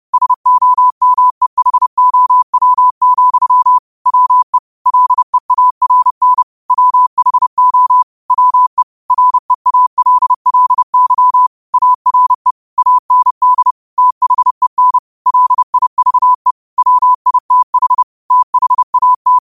Morse code, heard as a keyed tone, about 30 wpm, 1000 Hz. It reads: SOMEHOW, WE LEARN WHO WE REALLY ARE AND THEN LIVE WITH THAT